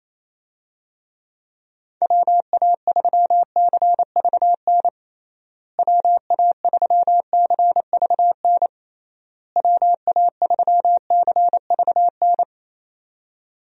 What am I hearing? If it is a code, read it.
Morse code, heard as a keyed tone, about 27 wpm, 700 Hz. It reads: WA3CVN WA3CVN WA3CVN
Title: USA Call Signs - 6 Characters - Repeated 3x - 28wpm